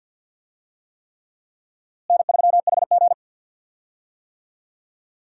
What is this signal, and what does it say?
Morse code, heard as a keyed tone, about 53 wpm, 700 Hz. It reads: N3HG
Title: USA Call Signs - 4 Characters - Repeated 3x - 50wpm